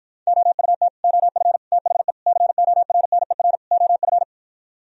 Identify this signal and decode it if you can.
Morse code, heard as a keyed tone, about 58 wpm, 700 Hz. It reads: OUT OF THE CORNER OF